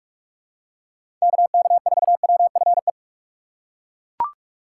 Morse code, heard as a keyed tone, about 41 wpm, 700 Hz. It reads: KK4WFE
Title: USA Call Signs - 6 Characters - 2x Character Spacing - 45wpm